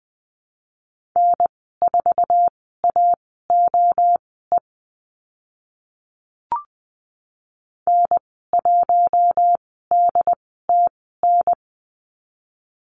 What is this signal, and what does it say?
Morse code, heard as a keyed tone, about 20 wpm, spaced as for 14 wpm, 700 Hz. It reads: N4AOE N1DTN
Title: 082 Lesson 14g - Number 2 - Call Signs 20wpm